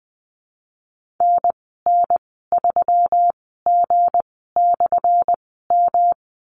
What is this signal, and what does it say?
Morse code, heard as a keyed tone, about 20 wpm, spaced as for 15 wpm, 700 Hz. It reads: NN3G/M
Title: 179 Lesson 29g - Slash - Call Signs 20wpm